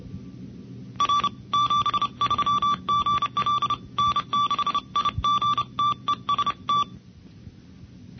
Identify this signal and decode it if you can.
Morse code, heard as a keyed tone, about 31 wpm, 1160 Hz. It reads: R 73 GL N6IG TEST
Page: NCCC SS Tape #2 - 1978 by: W1ARR, N6IG, N6NE and WA6HJV